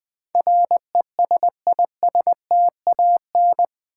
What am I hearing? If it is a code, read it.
Morse code, heard as a keyed tone, about 20 wpm, 700 Hz. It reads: RESISTAN
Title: Build A Word - 4 Letters At A Time - 10-Letter Words - 20wpm